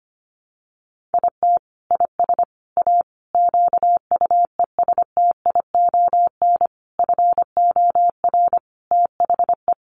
Morse code, heard as a keyed tone, about 25 wpm, 700 Hz. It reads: IT IS A QUESTION FOR THE
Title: Sentences from Top 1300 Words 25wpm